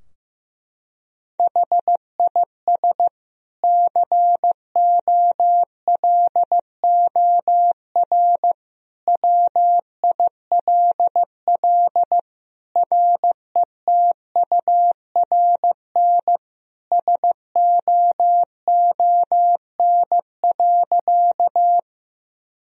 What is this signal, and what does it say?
Morse code, heard as a keyed tone, about 15 wpm, 700 Hz. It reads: HIS COLOR WILL RETURN SOON.
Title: Sentences from Top 1000 Words - Review Part 2 - 15wpm